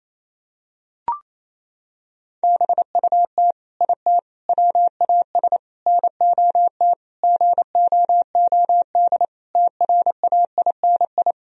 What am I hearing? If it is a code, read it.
Morse code, heard as a keyed tone, about 28 wpm, 700 Hz. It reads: BUT IT WAS NOT GOOD TRAINI
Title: Sentences from Top 1000 Words - Review Part 4 - 28wpm